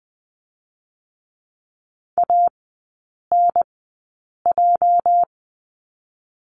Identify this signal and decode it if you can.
Morse code, heard as a keyed tone, about 20 wpm, 700 Hz. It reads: A N J
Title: 205 Lesson 33e - Letter j - Three Characters 20wpm